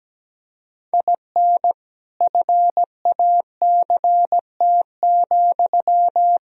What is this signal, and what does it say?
Morse code, heard as a keyed tone, about 17 wpm, 700 Hz. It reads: IN FACT,
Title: Sentences from Top 1000 Words - Review Part 3 - 17wpm